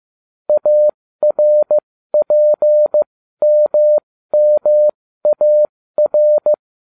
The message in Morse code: ARPMMAR

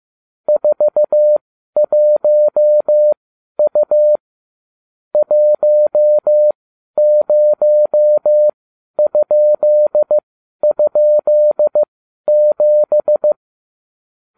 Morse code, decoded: 41U 10??7

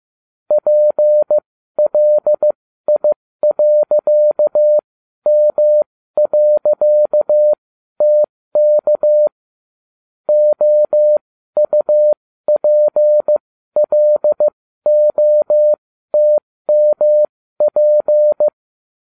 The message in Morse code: PLI.M.TK OUPLOTMP

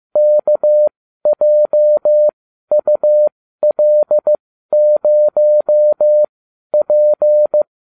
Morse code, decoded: KJUL0P

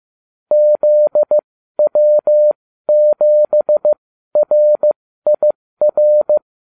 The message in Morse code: ZW7RIR

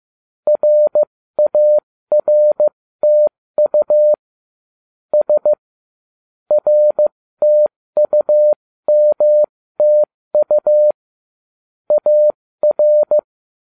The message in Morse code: RARTU S RTUMTU AR

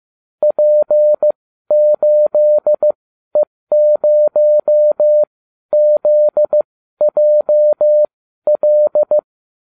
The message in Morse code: P8E0ZJL

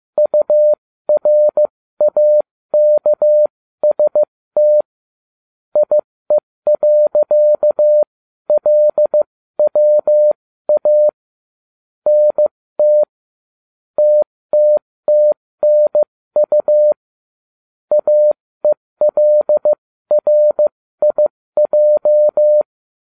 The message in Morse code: URAKST IE.LWA NT TTTNU AELRIJ

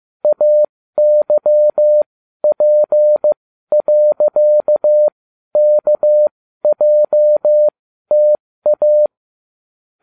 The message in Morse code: AYP.KJTA

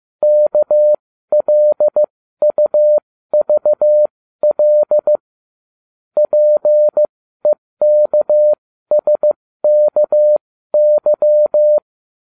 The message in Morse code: KLUVL PEKSKY

